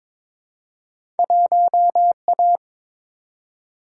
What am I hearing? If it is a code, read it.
Morse code, heard as a keyed tone, about 22 wpm, 700 Hz. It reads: 1A
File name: practice / 22wpm / Two Letter-Number Combinations - 22wpm